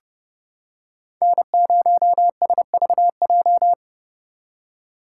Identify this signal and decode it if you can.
Morse code, heard as a keyed tone, about 31 wpm, 700 Hz. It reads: N0SVJ